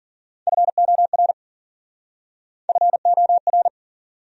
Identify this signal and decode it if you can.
Morse code, heard as a keyed tone, about 46 wpm, 700 Hz. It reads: FOR FOR
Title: Speed-Racing Top 200 Words 40wpm